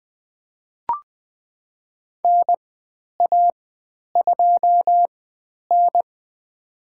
Morse code, heard as a keyed tone, about 20 wpm, spaced as for 10 wpm, 700 Hz. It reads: NA2N
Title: 109 Lesson 18g - Letter m - Call Signs 20wpm